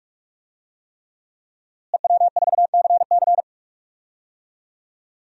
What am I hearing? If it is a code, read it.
Morse code, heard as a keyed tone, about 46 wpm, 700 Hz. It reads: EW4CC
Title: DXCC-100 Call Signs - 5 Characters - Repeated 3x - 45wpm